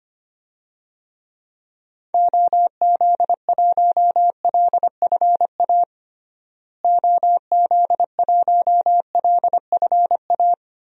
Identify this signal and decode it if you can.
Morse code, heard as a keyed tone, about 25 wpm, 700 Hz. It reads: OZ1LFA OZ1LFA
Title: DXCC-100 Call Signs - 6 Characters - Repeated 3x - 25wpm